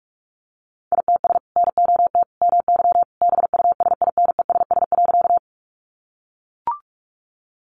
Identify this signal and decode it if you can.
Morse code, heard as a keyed tone, about 44 wpm, 700 Hz. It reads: ITS NOT MY BUSINESS.
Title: Sentences from Top 600 Words 45wpm